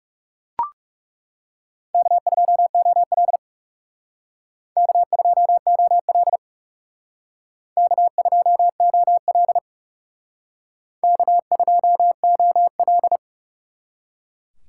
Morse code, heard as a keyed tone, about 36 wpm, 700 Hz. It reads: K2OL K2OL K2OL K2OL
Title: Speed Racing - USA 1x2 Call Signs 30wpm